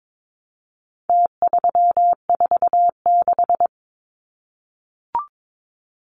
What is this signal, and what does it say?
Morse code, heard as a keyed tone, about 22 wpm, 700 Hz. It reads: T346
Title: Numbers - Contest Style - 4-Digit 22wpm